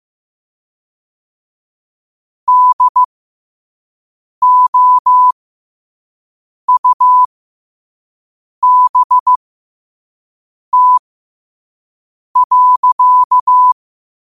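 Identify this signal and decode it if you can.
Morse code, heard as a keyed tone, about 15 wpm, spaced as for 5 wpm, 1000 Hz. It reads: DOUBT.
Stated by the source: Morse code quote-of-the-day practice at 5 wpm.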